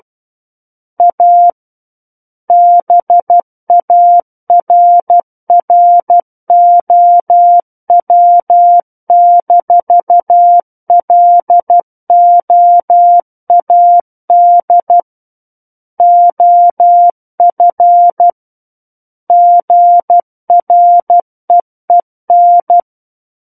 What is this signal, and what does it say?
Morse code, heard as a keyed tone, about 12 wpm, 700 Hz. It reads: A BARROW-LOAD OF GREEN
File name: Book One / 03-Chapter 3 - 12 WPM